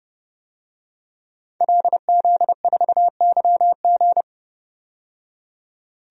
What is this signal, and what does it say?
Morse code, heard as a keyed tone, about 30 wpm, 700 Hz. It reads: LZ4YG
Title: DXCC-100 Call Signs - 5 Characters - 30wpm